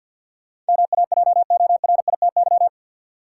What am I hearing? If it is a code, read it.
Morse code, heard as a keyed tone, about 53 wpm, 700 Hz. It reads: MAJORITY
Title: Sets of 3 Words - New Word Drill - 50wpm